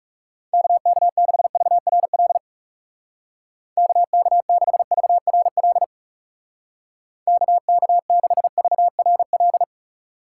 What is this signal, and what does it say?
Morse code, heard as a keyed tone, about 37 wpm, 700 Hz. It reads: KK6VRL KK6VRL KK6VRL
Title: Speed Racing - USA 2x3 Call Signs 30wpm